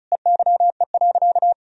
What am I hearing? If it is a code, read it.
Morse code, heard as a keyed tone, about 36 wpm, 700 Hz. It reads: EYE.
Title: Sentences from Top 1000 Words - Review Part 1 - 35wpm